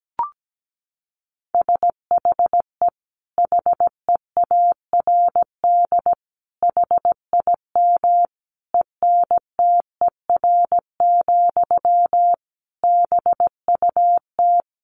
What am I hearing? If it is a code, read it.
Morse code, heard as a keyed tone, about 17 wpm, 700 Hz. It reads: SHE HEARD HIM ENTER, BUT
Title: Sentences from Top 1400 Words 17wpm